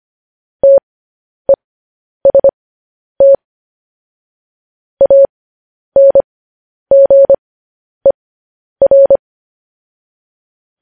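Morse code, heard as a keyed tone, about 25 wpm, spaced as for 10 wpm, 550 Hz. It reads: TEST ANGER
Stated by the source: Morse code news headlines practice at 10 wpm.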